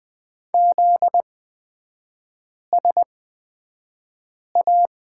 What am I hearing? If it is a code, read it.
Morse code, heard as a keyed tone, about 20 wpm, spaced as for 5 wpm, 700 Hz. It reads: ZSA
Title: 243 Lesson 39e - Letter z - Three Characters 20wpm